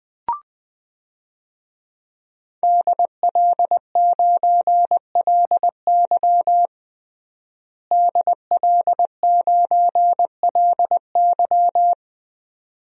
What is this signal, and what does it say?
Morse code, heard as a keyed tone, about 21 wpm, 700 Hz. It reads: DL9LY DL9LY
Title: DXCC-100 Call Signs - 5 Characters - Repeated 2x - 20wpm